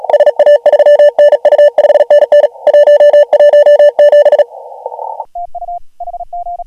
Morse code, heard as a keyed tone, about 37 wpm, 590 Hz. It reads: SA3NU 5NN 117
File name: AUDIO2019 / HG1G CW